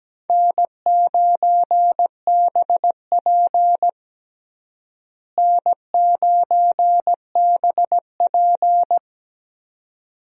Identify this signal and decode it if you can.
Morse code, heard as a keyed tone, about 17 wpm, 700 Hz. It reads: N9BP N9BP
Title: USA Call Signs - 4 Characters - Repeated 2x - 17wpm